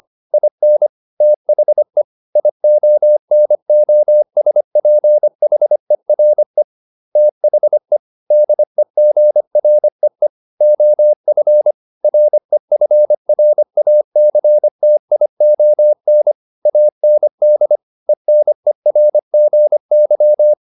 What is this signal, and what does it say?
Morse code, heard as a keyed tone, about 25 wpm, 600 Hz. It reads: IN THE IONOSPHERE THE DEGREE OF REFRACTION AND ENERGY